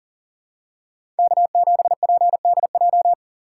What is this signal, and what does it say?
Morse code, heard as a keyed tone, about 40 wpm, 700 Hz. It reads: K7PDJ
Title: USA Call Signs - 5 Characters - 4x Character Spacing - 40wpm